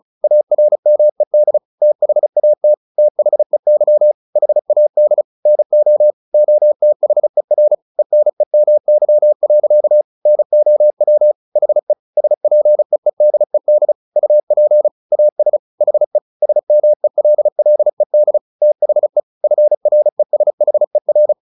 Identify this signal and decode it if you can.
Morse code, heard as a keyed tone, about 35 wpm, 600 Hz. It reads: ARMED THAT THEY HAD NO OTHER ENEMY. NOW HE SPEEDED UP AS HE SMELLED THE FRESHER
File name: the old man and the sea 2 35wpm